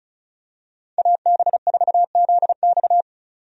Morse code, heard as a keyed tone, about 36 wpm, 700 Hz. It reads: AB4ZX
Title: USA Call Signs - 5 Characters - 35wpm